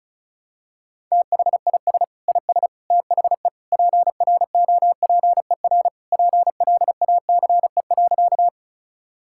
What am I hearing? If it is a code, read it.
Morse code, heard as a keyed tone, about 36 wpm, 700 Hz. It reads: THIS IS THE PROPER PLACE.
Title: Sentences from Top 1000 Words - Review Part 5 - 35wpm